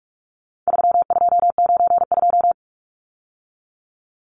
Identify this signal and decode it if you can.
Morse code, heard as a keyed tone, about 44 wpm, 700 Hz. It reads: 3292
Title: Numbers - 4-Digit 45wpm